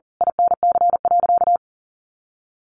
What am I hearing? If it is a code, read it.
Morse code, heard as a keyed tone, about 40 wpm, 700 Hz. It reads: INC.
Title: A Sound Of Thunder 40wpm